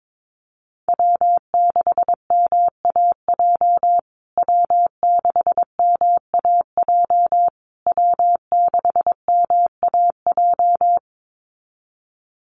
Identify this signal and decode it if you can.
Morse code, heard as a keyed tone, about 22 wpm, 700 Hz. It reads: W6MAJ W6MAJ W6MAJ
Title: USA Call Signs Repeated 3x 22wpm